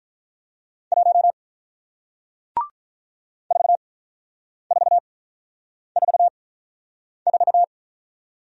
Morse code, heard as a keyed tone, about 48 wpm, spaced as for 9 wpm, 700 Hz. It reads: 1 4444